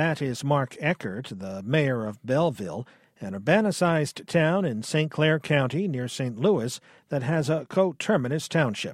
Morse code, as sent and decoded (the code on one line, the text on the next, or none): none